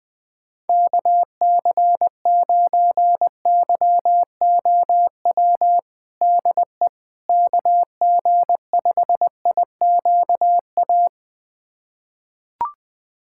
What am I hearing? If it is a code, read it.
Morse code, heard as a keyed tone, about 20 wpm, 700 Hz. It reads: KC9YOW DE KG5IQA
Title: USA Callsign DE Callsign - 6 Characters - 20wpm